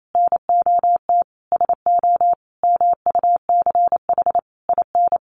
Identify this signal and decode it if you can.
Morse code, heard as a keyed tone, about 28 wpm, 700 Hz. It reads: NOT SO MUCH IN
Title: Sentences from Top 1500 Words 28wpm